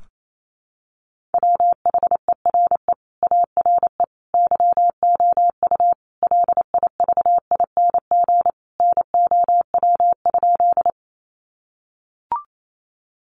Morse code, heard as a keyed tone, about 28 wpm, 700 Hz. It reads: WHERE ARE YOU LIVING NOW?
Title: Sentences from Top 1100 Words 28wpm